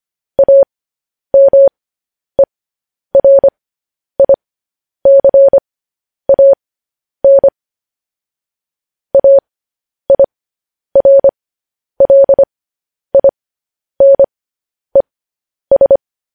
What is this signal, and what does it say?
Morse code, heard as a keyed tone, about 25 wpm, spaced as for 10 wpm, 550 Hz. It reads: AMERICAN AIRLINES
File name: podcast / News headlines-1x-25x10